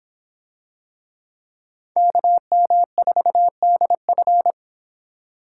Morse code, heard as a keyed tone, about 26 wpm, 700 Hz. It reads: KM4DF